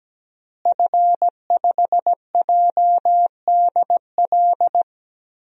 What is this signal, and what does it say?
Morse code, heard as a keyed tone, about 17 wpm, 700 Hz. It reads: F5JDL
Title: DXCC-100 Call Signs - 5 Characters - Repeated 2x - 17wpm